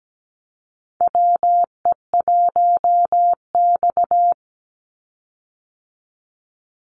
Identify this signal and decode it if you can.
Morse code, heard as a keyed tone, about 17 wpm, 700 Hz. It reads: WE1X
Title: USA Call Signs 17wpm